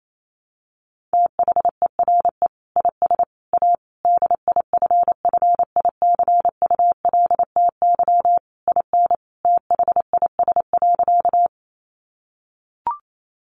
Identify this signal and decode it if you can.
Morse code, heard as a keyed tone, about 28 wpm, 700 Hz. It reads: THERE IS A DIFFICULTY IN THIS.
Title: Sentences from Top 1500 Words 28wpm